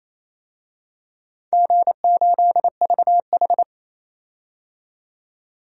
Morse code, heard as a keyed tone, about 28 wpm, 700 Hz. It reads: G8VH
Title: DXCC-100 Call Signs - 4 Characters - 28wpm